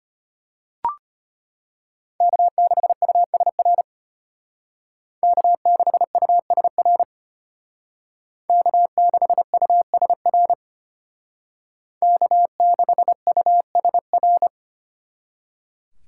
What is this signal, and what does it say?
Morse code, heard as a keyed tone, about 32 wpm, 700 Hz. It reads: K6USR K6USR K6USR K6USR